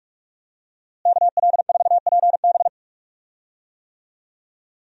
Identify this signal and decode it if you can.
Morse code, heard as a keyed tone, about 42 wpm, 700 Hz. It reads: KL4PB
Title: USA Call Signs - 5 Characters - Repeated 3x - 45wpm